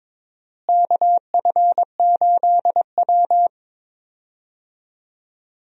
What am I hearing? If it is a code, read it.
Morse code, heard as a keyed tone, about 23 wpm, 700 Hz. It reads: KF8W
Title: USA Call Signs - 4 Characters - Repeated 3x - 22wpm